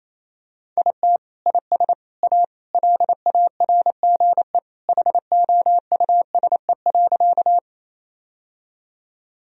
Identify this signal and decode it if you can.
Morse code, heard as a keyed tone, about 28 wpm, 700 Hz. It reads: IT IS A LARGE HOUSE.
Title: Sentences from Top 1000 Words - Review Part 1 - 28wpm